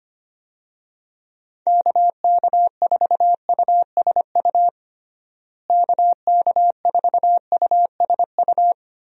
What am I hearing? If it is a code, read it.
Morse code, heard as a keyed tone, about 25 wpm, 700 Hz. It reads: KK4USU KK4USU
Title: USA Call Signs - 6 Characters - Repeated 3x - 25wpm